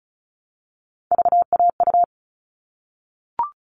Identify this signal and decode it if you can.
Morse code, heard as a keyed tone, about 34 wpm, 700 Hz. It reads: VAU
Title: All Cut Numbers - 3-Digit 35wpm